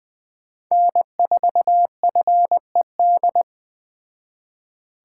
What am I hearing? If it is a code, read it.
Morse code, heard as a keyed tone, about 20 wpm, 700 Hz. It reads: N4FED
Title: Speed Racing - USA Call Signs 17wpm